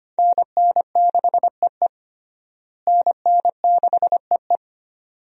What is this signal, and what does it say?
Morse code, heard as a keyed tone, about 25 wpm, 700 Hz. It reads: NN6EE NN6EE
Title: USA Call Signs - 5 Characters - Repeated 2x - 25wpm